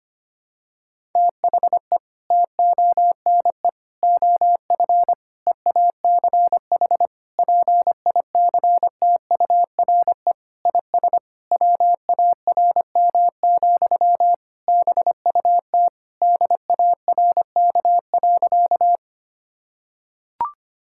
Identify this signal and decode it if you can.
Morse code, heard as a keyed tone, about 25 wpm, 700 Hz. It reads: THE TONE OF EACH PICTURE IS WARM, BUT DARK.
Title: Sentences from Top 1600 Words 25wpm